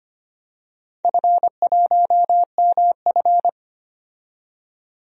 Morse code, heard as a keyed tone, about 25 wpm, 700 Hz. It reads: F1MF
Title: DXCC-100 Call Signs - 4 Characters - 25wpm